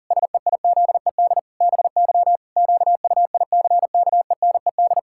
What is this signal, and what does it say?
Morse code, heard as a keyed tone, about 40 wpm, 700 Hz. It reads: SEIZED BY QUICKENED
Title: Pangram Challenge v2 - 40wpm